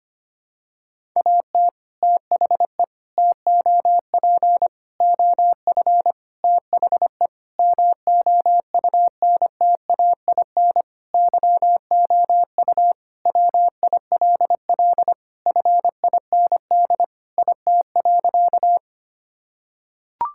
Morse code, heard as a keyed tone, about 25 wpm, 700 Hz. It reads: AT THE TOP OF THE MOUNTAIN YOU WILL FIND IT.
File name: practice / 25wpm / Sentences with 3-Word Phrases - 25wpm